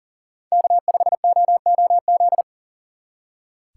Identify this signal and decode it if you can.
Morse code, heard as a keyed tone, about 42 wpm, 700 Hz. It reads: K5OOZ
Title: Speed Racing - USA 1x3 Call Signs 40wpm